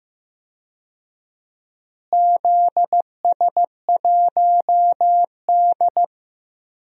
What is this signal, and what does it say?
Morse code, heard as a keyed tone, about 15 wpm, 700 Hz. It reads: ZS1D
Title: DXCC-100 Call Signs - 4 Characters - Repeated 2x - 15wpm